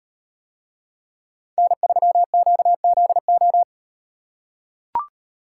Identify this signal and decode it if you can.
Morse code, heard as a keyed tone, about 36 wpm, 700 Hz. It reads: N3QZO